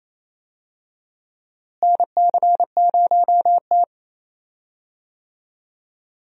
Morse code, heard as a keyed tone, about 28 wpm, 700 Hz. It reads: NC0T
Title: USA Call Signs - 4 Characters - Repeated 3x - 28wpm